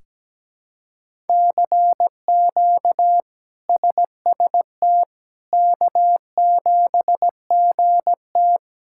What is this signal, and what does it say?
Morse code, heard as a keyed tone, about 17 wpm, 700 Hz. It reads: CQ SST K7GT
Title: K1USN SST QSO Practice - 17wpm